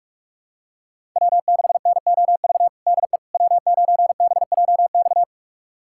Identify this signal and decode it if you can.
Morse code, heard as a keyed tone, about 41 wpm, 700 Hz. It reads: W6NOV DE W9BJX
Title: USA Callsign DE Callsign - 5 Characters - 45wpm